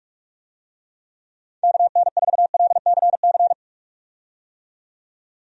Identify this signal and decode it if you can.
Morse code, heard as a keyed tone, about 46 wpm, 700 Hz. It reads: KN4LCC
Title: USA Call Signs - 6 Characters - 45wpm